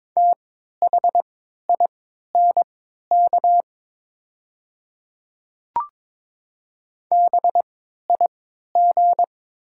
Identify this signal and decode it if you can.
Morse code, heard as a keyed tone, about 22 wpm, spaced as for 12 wpm, 700 Hz. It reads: THINK BIG
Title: 3x Character Spacing - Top 300 Words - 22wpm